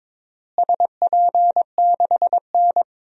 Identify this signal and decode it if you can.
Morse code, heard as a keyed tone, about 21 wpm, 700 Hz. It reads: SP6N
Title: DXCC-100 Call Signs - 4 Characters - Repeated 3x - 22wpm